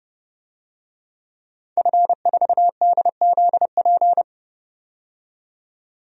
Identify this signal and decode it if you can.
Morse code, heard as a keyed tone, about 30 wpm, 700 Hz. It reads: F4DZP
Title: DXCC-100 Call Signs - 5 Characters - 30wpm